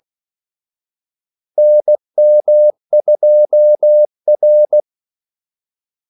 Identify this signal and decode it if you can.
Morse code, heard as a keyed tone, about 16 wpm, 600 Hz. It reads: NM2R
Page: Kaldesignaler 16 wpm